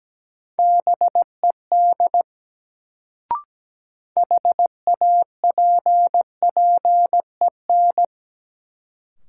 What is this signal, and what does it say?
Morse code, heard as a keyed tone, about 17 wpm, 700 Hz. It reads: BED HAPPEN
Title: Sets of 5 Words Encore - New Word Drill - 17wpm